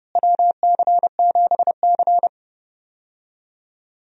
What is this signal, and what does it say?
Morse code, heard as a keyed tone, about 32 wpm, 700 Hz. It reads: WC7C
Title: USA Call Signs - 4 Characters - Repeated 3x - 30wpm